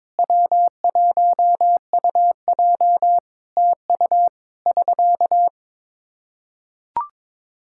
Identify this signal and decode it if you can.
Morse code, heard as a keyed tone, about 22 wpm, 700 Hz. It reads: W1UJ TU <SK>
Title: Practice QSO - Exchange 4 - Ending - 22wpm